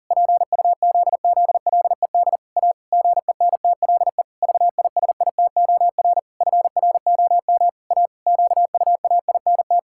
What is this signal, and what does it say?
Morse code, heard as a keyed tone, about 40 wpm, 700 Hz. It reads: PUZZLED A GENTLE VISITOR FROM A QUAINT
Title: Pangram Challenge v2 - 40wpm